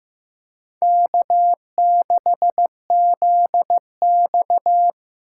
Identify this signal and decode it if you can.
Morse code, heard as a keyed tone, about 15 wpm, 700 Hz. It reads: K6ZX
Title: K1USN SST QSO Practice - 15wpm